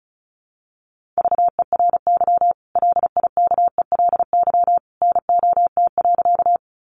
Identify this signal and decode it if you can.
Morse code, heard as a keyed tone, about 34 wpm, 700 Hz. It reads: VERY LIKELY NOT.
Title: Sentences from Top 600 Words 35wpm